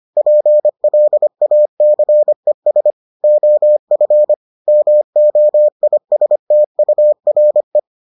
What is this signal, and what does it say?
Morse code, heard as a keyed tone, about 25 wpm, 600 Hz. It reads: PLACES OF MOISTURE